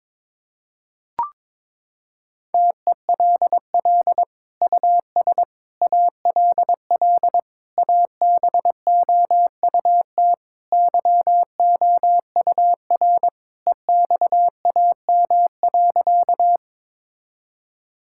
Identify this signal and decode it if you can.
Morse code, heard as a keyed tone, about 21 wpm, 700 Hz. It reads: TELL US ALL ABOUT YOUR EXAM.
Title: Sentences from Top 1000 Words - Review Part 5 - 22wpm